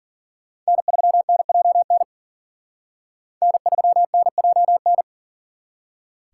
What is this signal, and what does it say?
Morse code, heard as a keyed tone, about 44 wpm, 700 Hz. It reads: N3NJN N3NJN
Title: Speed Racing - USA 1x3 Call Signs 40wpm